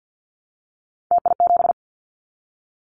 Morse code, heard as a keyed tone, about 50 wpm, 700 Hz. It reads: TI7
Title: Callsign Prefixes for Top 100 DXCC - 50wpm